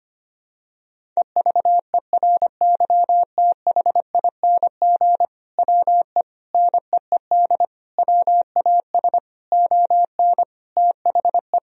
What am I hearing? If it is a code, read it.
Morse code, heard as a keyed tone, about 25 wpm, 700 Hz. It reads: EVERYTHING WE NEED WAS ON THE
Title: Sentences from Top 1000 Words - Review Part 1 - 25wpm